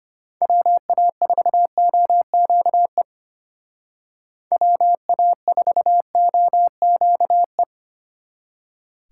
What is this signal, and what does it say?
Morse code, heard as a keyed tone, about 30 wpm, 700 Hz. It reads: WA4OQE WA4OQE
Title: Speed Racing - USA Call Signs 25wpm